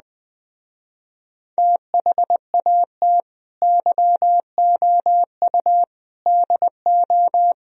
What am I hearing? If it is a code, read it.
Morse code, heard as a keyed tone, about 20 wpm, 700 Hz. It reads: THAT YOU DO
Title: Speed-Racing - Sets of 4 Words 17wpm